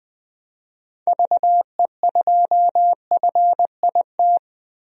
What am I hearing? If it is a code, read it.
Morse code, heard as a keyed tone, about 20 wpm, 700 Hz. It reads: VE2FIT
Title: DXCC-100 Call Signs - 6 Characters - 20wpm